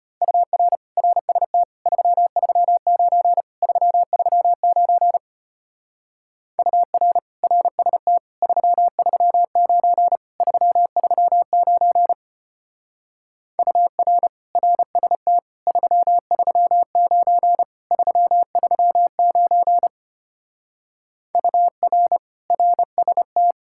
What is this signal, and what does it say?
Morse code, heard as a keyed tone, about 33 wpm, 700 Hz. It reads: UR RST 339 339 UR RST 339 339 UR RST 339 339 UR RST